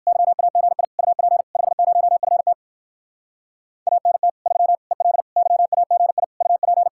Morse code, heard as a keyed tone, about 56 wpm, 700 Hz. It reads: YAGI UP 50FT ANT 3 EL YAGI UP
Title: Speed-Racing Common QSO Phrases - 45wpm